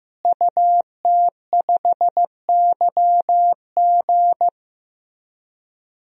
Morse code, heard as a keyed tone, about 15 wpm, 700 Hz. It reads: UT5YG
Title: DXCC-100 Call Signs - 5 Characters - Repeated 3x - 15wpm